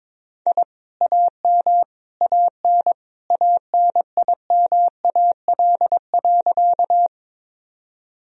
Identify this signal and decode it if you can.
Morse code, heard as a keyed tone, about 21 wpm, 700 Hz. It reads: I AM AN ANIMAL.
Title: Sentences from Top 1000 Words - Review Part 1 - 22wpm